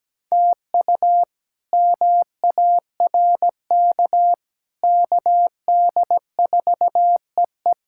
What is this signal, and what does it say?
Morse code, heard as a keyed tone, about 17 wpm, 700 Hz. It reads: TU MARK KD4EE
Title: K1USN SST QSO Practice - 17wpm